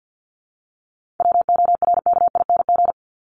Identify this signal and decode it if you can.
Morse code, heard as a keyed tone, about 50 wpm, 700 Hz. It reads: WORKING
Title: ING Suffix - Top 100 Words with Base Words 50wpm